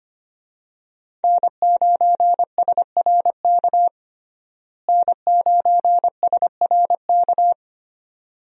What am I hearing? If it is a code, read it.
Morse code, heard as a keyed tone, about 25 wpm, 700 Hz. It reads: N9SRK N9SRK
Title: USA Call Signs - 5 Characters - Repeated 2x - 25wpm